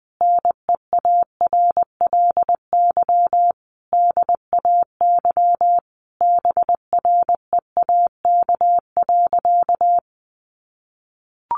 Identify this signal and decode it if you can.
Morse code, heard as a keyed tone, about 20 wpm, 700 Hz. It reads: NEARLY DAY BREAK.
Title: Sentences from Top 1300 Words 20wpm